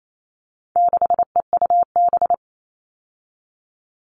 Morse code, heard as a keyed tone, about 28 wpm, 700 Hz. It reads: 6EUB
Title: All Cut Numbers - 4-Digit 28wpm